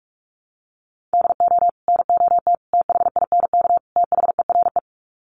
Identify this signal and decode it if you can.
Morse code, heard as a keyed tone, about 44 wpm, 700 Hz. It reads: DO NOT THINK THERE
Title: Sets of 4 Words 45wpm